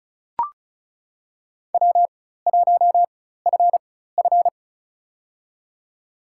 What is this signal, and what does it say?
Morse code, heard as a keyed tone, about 36 wpm, spaced as for 16 wpm, 700 Hz. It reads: W1FF
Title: USA Call Signs - 4x Character Spacing - 35wpm